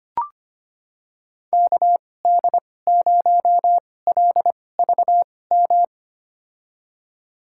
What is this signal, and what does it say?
Morse code, heard as a keyed tone, about 25 wpm, spaced as for 18 wpm, 700 Hz. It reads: KD0LVM
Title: USA Call Signs - 6 Characters - 2x Character Spacing - 25wpm